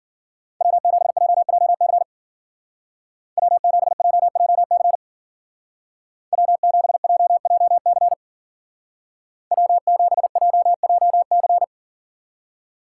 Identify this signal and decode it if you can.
Morse code, heard as a keyed tone, about 50 wpm, 700 Hz. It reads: W7JJC W7JJC W7JJC W7JJC